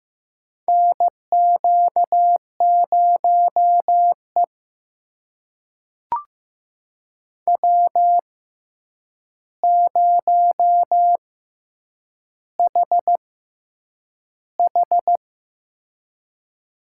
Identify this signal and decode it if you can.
Morse code, heard as a keyed tone, about 15 wpm, spaced as for 8 wpm, 700 Hz. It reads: NQ0E W0HH
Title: USA Call Signs - 4 Characters - 6x Character Spacing - 15wpm